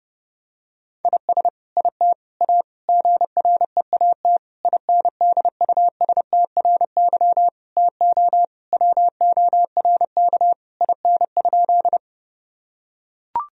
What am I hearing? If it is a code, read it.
Morse code, heard as a keyed tone, about 30 wpm, 700 Hz. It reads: IS IT A GREAT INDUSTRY TO WORK IN?
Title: Sentences from Top 1000 Words - Review Part 4 - 30wpm